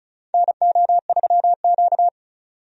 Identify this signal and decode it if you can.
Morse code, heard as a keyed tone, about 36 wpm, 700 Hz. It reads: NO3Q
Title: USA Call Signs - 4 Characters - 35wpm